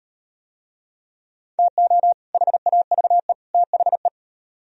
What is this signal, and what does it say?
Morse code, heard as a keyed tone, about 36 wpm, 700 Hz. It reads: TO HAVE THE